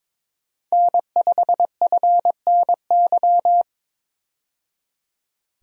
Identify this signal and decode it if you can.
Morse code, heard as a keyed tone, about 22 wpm, 700 Hz. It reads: N5FNY
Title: USA Call Signs - 5 Characters - 22wpm